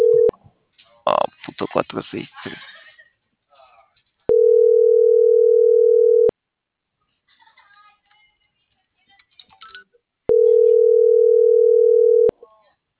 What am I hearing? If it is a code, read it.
Ringback tone (North America), 440 + 480 Hz, with speech.
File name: CENTRAL SAGOT